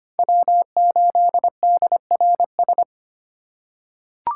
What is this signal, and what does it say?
Morse code, heard as a keyed tone, about 25 wpm, 700 Hz. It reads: W8DRS
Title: USA Call Signs - 5 Characters - 6x Character Spacing - 25wpm